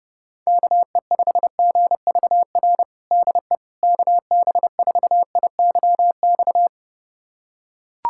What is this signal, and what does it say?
Morse code, heard as a keyed tone, about 32 wpm, 700 Hz. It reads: KE5GVR DE KB4IYX
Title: USA Callsign DE Callsign 30wpm